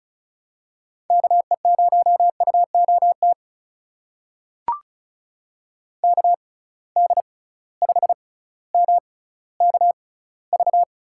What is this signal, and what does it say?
Morse code, heard as a keyed tone, about 36 wpm, spaced as for 18 wpm, 700 Hz. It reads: KE0UOT KD5MKV
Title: USA Call Signs - 6 Characters - 6x Character Spacing - 35wpm